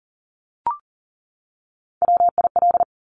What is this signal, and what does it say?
Morse code, heard as a keyed tone, about 41 wpm, 700 Hz. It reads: WIL
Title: Top Three Letter Combinations in Words 40wpm